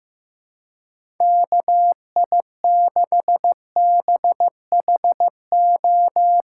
Morse code, heard as a keyed tone, about 15 wpm, 700 Hz. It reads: KI6BHO
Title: USA Callsign DE Callsign 15wpm